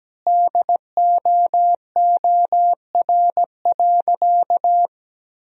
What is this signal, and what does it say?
Morse code, heard as a keyed tone, about 17 wpm, 700 Hz. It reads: DOOR.
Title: 4x Word-Spacing - Sentences from Top 400 Words - 17wpm